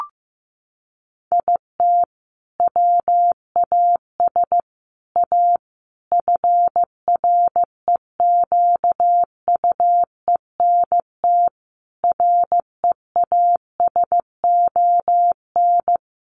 Morse code, heard as a keyed tone, about 15 wpm, 700 Hz. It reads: IT WAS A FREQUENT REASON
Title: Sentences from Top 1400 Words 15wpm